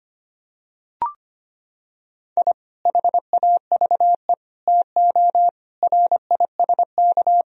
Morse code, heard as a keyed tone, about 25 wpm, 700 Hz. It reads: I HAVE TO RISK
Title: Sentences from Top 1000 Words - Review Part 4 - 25wpm